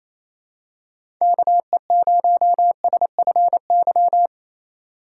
Morse code, heard as a keyed tone, about 28 wpm, 700 Hz. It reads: KE0SFY
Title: USA Call Signs - 6 Characters - 4x Character Spacing - 28wpm